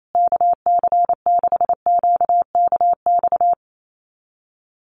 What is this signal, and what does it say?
Morse code, heard as a keyed tone, about 28 wpm, 690 Hz. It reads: KC6QKX
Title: USA Call Signs Repeated 3x 28wpm